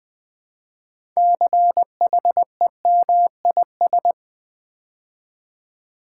Morse code, heard as a keyed tone, about 20 wpm, 700 Hz. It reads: CHEMIS
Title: Build A Word - 3 Letters At A Time - 9-Letter Words - 20wpm